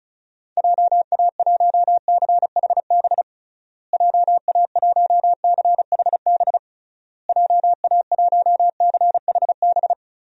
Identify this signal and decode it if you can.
Morse code, heard as a keyed tone, about 34 wpm, 700 Hz. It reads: JA1CHB JA1CHB JA1CHB
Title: DXCC-100 Call Signs - 6 Characters - Repeated 3x - 35wpm